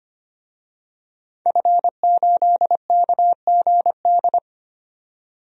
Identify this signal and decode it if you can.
Morse code, heard as a keyed tone, about 25 wpm, 700 Hz. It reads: F8KGD
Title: DXCC-100 Call Signs - 5 Characters - 25wpm